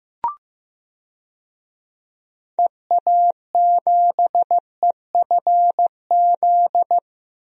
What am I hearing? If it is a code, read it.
Morse code, heard as a keyed tone, about 16 wpm, 700 Hz. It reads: EA7EFZ
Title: DXCC-100 Call Signs - 6 Characters - Repeated 2x - 15wpm